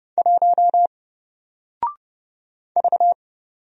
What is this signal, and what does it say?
Morse code, heard as a keyed tone, about 30 wpm, 700 Hz. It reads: 1 V
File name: Single-Letter-Number-30wpm